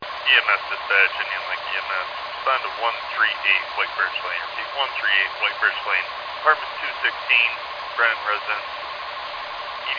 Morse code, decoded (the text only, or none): none